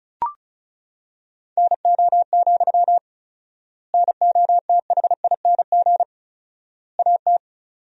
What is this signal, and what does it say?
Morse code, heard as a keyed tone, about 34 wpm, 700 Hz. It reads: NO, NOTHING AT
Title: 4x Word-Spacing - Sentences from Top 500 Words - 35wpm